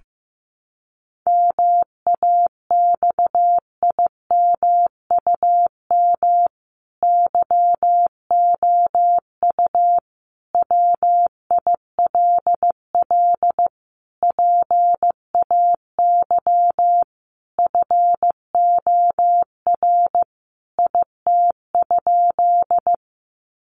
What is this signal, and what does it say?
Morse code, heard as a keyed tone, about 15 wpm, 700 Hz. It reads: MAXIMUM YOU WILL PAY FOR IT?
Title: Sentences from Top 1300 Words 15wpm